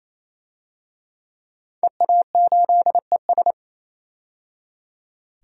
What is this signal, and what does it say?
Morse code, heard as a keyed tone, about 28 wpm, 700 Hz. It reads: EA8ES
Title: DXCC-100 Call Signs - 5 Characters - 28wpm